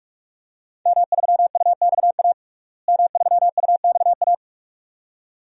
Morse code, heard as a keyed tone, about 46 wpm, 700 Hz. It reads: M3UXA M3UXA
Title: DXCC-100 Call Signs - 5 Characters - Repeated 2x - 45wpm